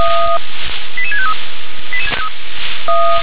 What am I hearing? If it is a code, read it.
Five-tone selective call (ZVEI), heard as quick runs of five tones, tones of about 70 ms. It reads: 88653
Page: /357/SDR/ZVEI/88154/